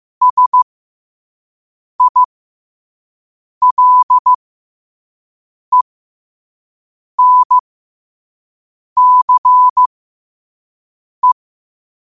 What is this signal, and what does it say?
Morse code, heard as a keyed tone, about 15 wpm, spaced as for 5 wpm, 1000 Hz. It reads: SILENCE
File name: qotdfiles / 5wpm